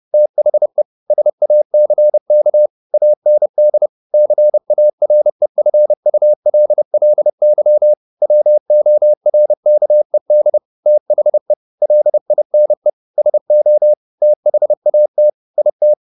Morse code, heard as a keyed tone, about 30 wpm, 600 Hz. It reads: THE SACK AND CAREFULLY WORKED THE LINE SO THAT IT
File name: the old man and the sea / the old man and the sea 1 30wpm